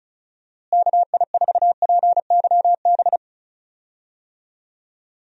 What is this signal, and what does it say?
Morse code, heard as a keyed tone, about 34 wpm, 700 Hz. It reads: KI4PYB
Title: USA Call Signs - 6 Characters - Repeated 3x - 35wpm